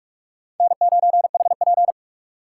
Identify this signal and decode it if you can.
Morse code, heard as a keyed tone, about 45 wpm, 700 Hz. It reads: N9HP